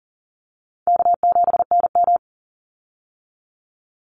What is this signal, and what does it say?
Morse code, heard as a keyed tone, about 40 wpm, 700 Hz. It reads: K7NM
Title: Easy Call Signs Repeated 2x - 40wpm